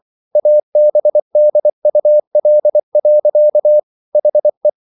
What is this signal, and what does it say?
Morse code, heard as a keyed tone, about 24 wpm, 600 Hz. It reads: ABDUL. HE